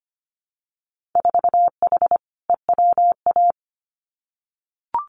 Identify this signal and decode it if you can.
Morse code, heard as a keyed tone, about 25 wpm, 700 Hz. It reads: 4H EWA
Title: Winter Field Day Exchanges Encore 25wpm